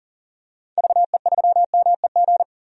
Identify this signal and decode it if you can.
Morse code, heard as a keyed tone, about 40 wpm, 700 Hz. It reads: VE3MEG
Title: DXCC-100 Call Signs - 6 Characters - 40wpm